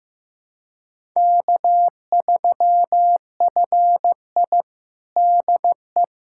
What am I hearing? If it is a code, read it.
Morse code, heard as a keyed tone, about 15 wpm, 700 Hz. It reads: K3FI DE
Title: USA Callsign DE Callsign - 4 Characters - 15wpm